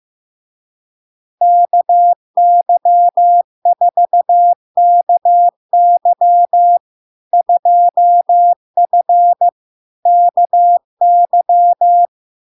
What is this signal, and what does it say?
Morse code, heard as a keyed tone, about 15 wpm, 700 Hz. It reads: KY4KY 2F KY
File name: LARK-FD-2022-Log-in-CW-15wpm-700-Hz